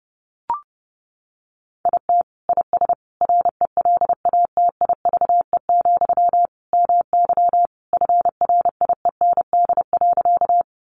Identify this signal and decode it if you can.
Morse code, heard as a keyed tone, about 30 wpm, 700 Hz. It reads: IT IS RELATIVE, MY FRIEND.
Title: Sentences from Top 1400 Words 30wpm